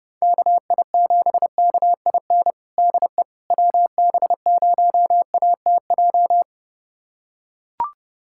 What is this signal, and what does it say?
Morse code, heard as a keyed tone, about 32 wpm, 700 Hz. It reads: KI7KIN DE WB0ATJ
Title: USA Callsign DE Callsign 30wpm